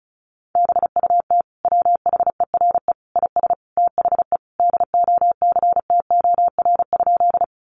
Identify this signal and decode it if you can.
Morse code, heard as a keyed tone, about 34 wpm, 700 Hz. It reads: BUT WHERE IS THE DOCTOR?
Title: Sentences from Top 1500 Words 35wpm